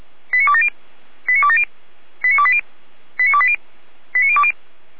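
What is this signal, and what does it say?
Five-tone selective call (ZVEI), heard as quick runs of five tones, tones of about 70 ms. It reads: 89270 89270 89280 89280 89020
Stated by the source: Five-tone ZVEI selective call received by SDR.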